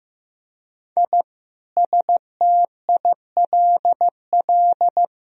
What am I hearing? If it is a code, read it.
Morse code, heard as a keyed tone, about 15 wpm, 700 Hz. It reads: I STILL
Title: Sentences from Top 1000 Words - Review Part 1 - 15wpm